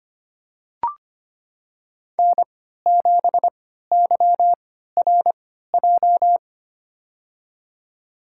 Morse code, heard as a keyed tone, about 25 wpm, spaced as for 14 wpm, 700 Hz. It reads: N7YRJ
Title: USA Call Signs - 3x Character Spacing - 25wpm